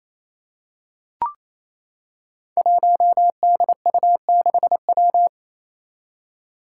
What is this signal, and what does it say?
Morse code, heard as a keyed tone, about 28 wpm, 700 Hz. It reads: 1DU6W
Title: Five Challenging Letter-Number Combinations 28wpm